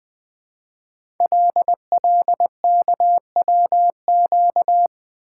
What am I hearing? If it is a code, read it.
Morse code, heard as a keyed tone, about 20 wpm, 700 Hz. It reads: LLKWQ
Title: Code Groups - 5 Letter:Numbers - 20wpm